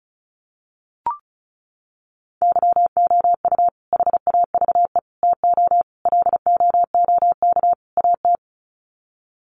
Morse code, heard as a keyed tone, about 34 wpm, 700 Hz. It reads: YOU HAVE TO LOOK AT
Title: Sets of 5 Words Encore 35wpm